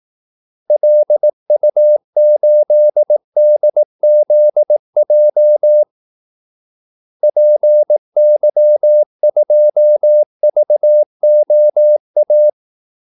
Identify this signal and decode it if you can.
Morse code, heard as a keyed tone, about 18 wpm, 600 Hz. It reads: LU8DZJ PY2VOA
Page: Kaldesignaler 18 wpm